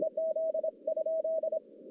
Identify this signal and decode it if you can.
Morse code, heard as a keyed tone, about 26 wpm, 640 Hz. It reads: EZ ?
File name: contests / CW-SS-k5tr- -1200